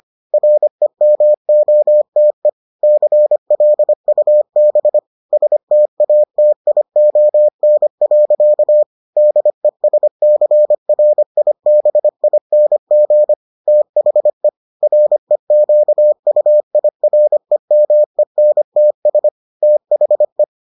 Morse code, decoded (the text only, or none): REMOTE CLUB STATION. DESCRIBING THE REQUIREMENTS THE